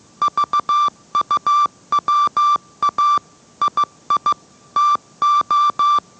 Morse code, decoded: VUWA II TO

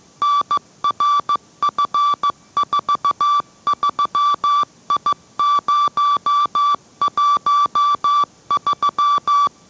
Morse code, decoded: NRF43I013

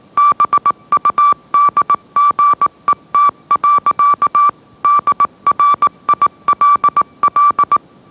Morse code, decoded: BUDGET. DRILL

